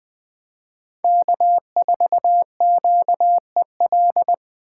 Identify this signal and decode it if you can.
Morse code, heard as a keyed tone, about 20 wpm, 700 Hz. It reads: K4QEL